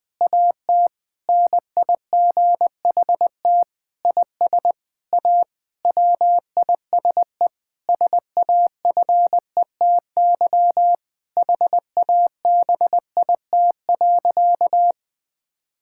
Morse code, heard as a keyed tone, about 20 wpm, 700 Hz. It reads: AT NIGHT IS A WISE SAFETY HABIT.
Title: Difficult Sentences with Hints - 20wpm